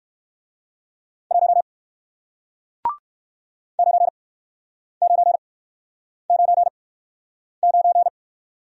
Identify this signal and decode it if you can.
Morse code, heard as a keyed tone, about 60 wpm, spaced as for 9 wpm, 700 Hz. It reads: 1 9999